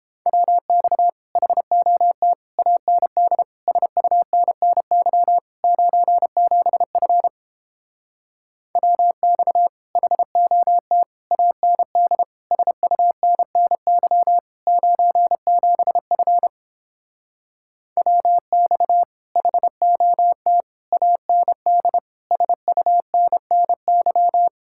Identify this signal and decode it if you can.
Morse code, heard as a keyed tone, about 29 wpm, 700 Hz. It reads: WX HOT AND SUNNY 97F WX HOT AND SUNNY 97F WX HOT AND SUNNY